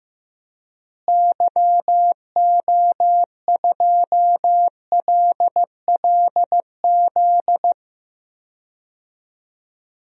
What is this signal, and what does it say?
Morse code, heard as a keyed tone, about 15 wpm, 700 Hz. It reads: YO2LLZ
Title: DXCC-100 Call Signs - 6 Characters - Repeated 2x - 15wpm